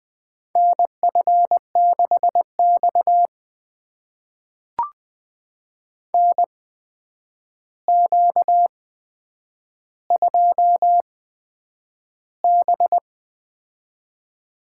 Morse code, heard as a keyed tone, about 21 wpm, spaced as for 9 wpm, 700 Hz. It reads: NF6X NQ2B
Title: USA Call Signs - 4 Characters - 8x Character Spacing - 20wpm